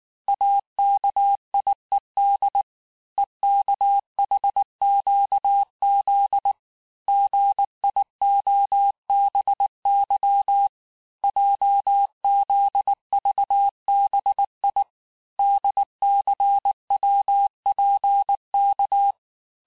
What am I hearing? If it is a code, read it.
Morse code, heard as a keyed tone, about 19 wpm, 800 Hz. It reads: AKIED EKHQZ GIOBY JZVBI DCWPK
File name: cours-du-lundi files / cours-1583--texte-chiffre-18-mpm